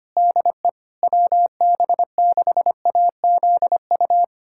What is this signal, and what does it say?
Morse code, heard as a keyed tone, about 25 wpm, 700 Hz. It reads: DE WB6AZU
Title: USA Callsign DE Callsign - 6 Characters - 25wpm